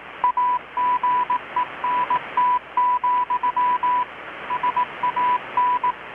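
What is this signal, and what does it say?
Morse code, heard as a keyed tone, about 18 wpm, 990 Hz, with radio noise. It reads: AGENT, SAN
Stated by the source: Morse from KPH's Night of Nights XX maritime radiotelegraphy event, received in Chiapas, Mexico about 3,800 km away on a Kenwood R-600 with a grounded 10 m long-wire antenna.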